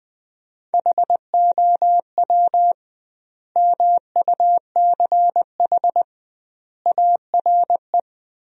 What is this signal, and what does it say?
Morse code, heard as a keyed tone, about 20 wpm, 700 Hz. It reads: HOW MUCH ARE
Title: 2x Word-Spacing - Sentences from Top 200 Words - 20wpm